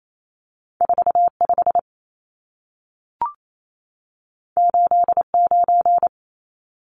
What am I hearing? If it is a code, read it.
Morse code, heard as a keyed tone, about 28 wpm, 700 Hz. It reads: 45 89
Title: Numbers - 2-Digit 28wpm